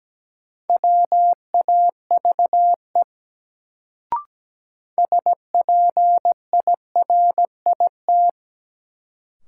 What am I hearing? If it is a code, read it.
Morse code, heard as a keyed tone, about 17 wpm, 700 Hz. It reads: WAVE SPIRIT
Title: Sentences from Top 1400 Words - New Word Drill - 17wpm